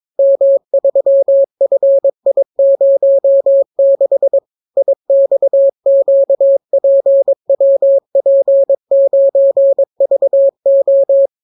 Morse code, heard as a keyed tone, about 22 wpm, 550 Hz. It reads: M3FI06 IXQPWP9VO